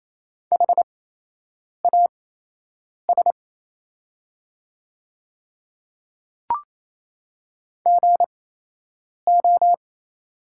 Morse code, heard as a keyed tone, about 28 wpm, spaced as for 8 wpm, 700 Hz. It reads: HAS GO
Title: 8x Character Spacing - Top 100 Words - 28wpm